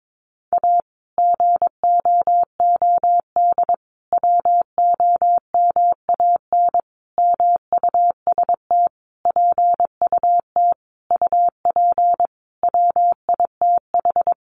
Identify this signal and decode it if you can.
Morse code, heard as a keyed tone, about 22 wpm, 700 Hz. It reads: A GOOD WOMAN MUST PUT UP WITH
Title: Sentences from Top 1100 Words 22wpm